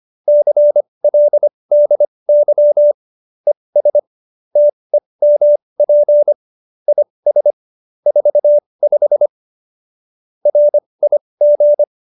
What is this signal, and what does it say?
Morse code, heard as a keyed tone, about 25 wpm, spaced as for 20 wpm, 590 Hz. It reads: CLDY ES TEMP IS 45 RIG
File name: qso220b 20